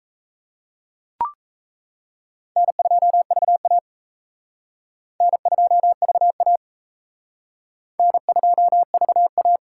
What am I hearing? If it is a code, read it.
Morse code, heard as a keyed tone, about 35 wpm, 700 Hz. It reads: N2VA N2VA N2VA
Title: Speed Racing - USA 1x2 Call Signs 28wpm